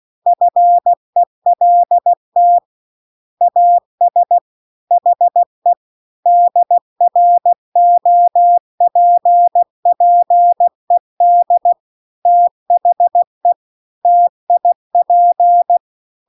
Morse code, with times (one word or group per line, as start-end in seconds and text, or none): 0.3-2.6 FELT
3.4-4.4 AS
4.9-5.7 HE
6.3-11.7 DROPPED
12.3-13.5 THE
14.1-15.8 TIP